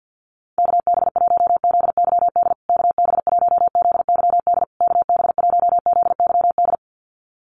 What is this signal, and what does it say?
Morse code, heard as a keyed tone, about 48 wpm, 700 Hz. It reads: KB1ZYD KB1ZYD KB1ZYD
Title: USA Call Signs Repeated 3x 50wpm